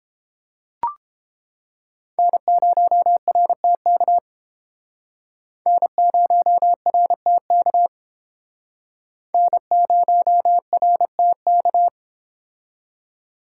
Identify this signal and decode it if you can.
Morse code, heard as a keyed tone, about 30 wpm, 700 Hz. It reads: N0RTK N0RTK N0RTK